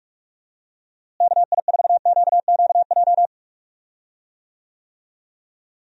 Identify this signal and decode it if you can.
Morse code, heard as a keyed tone, about 46 wpm, 700 Hz. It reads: KI4QQJ
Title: USA Call Signs - 6 Characters - Repeated 3x - 45wpm